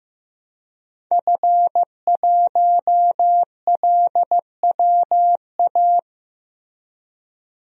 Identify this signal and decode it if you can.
Morse code, heard as a keyed tone, about 16 wpm, 700 Hz. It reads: F1LWA
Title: DXCC-100 Call Signs - 5 Characters - 15wpm